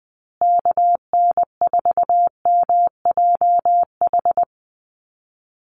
Morse code, heard as a keyed tone, about 20 wpm, 700 Hz. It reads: KN4MJH
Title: USA Call Signs Repeated 2x 20wpm